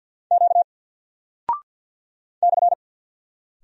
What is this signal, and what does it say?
Morse code, heard as a keyed tone, about 49 wpm, 700 Hz. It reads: Q /
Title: Single Character (Letter:Number:Punctuation:Prosign) 50wpm